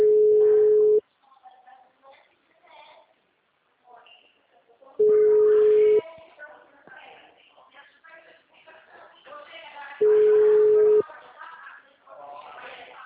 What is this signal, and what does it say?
Ringback tone (Europe, 425 Hz), with speech.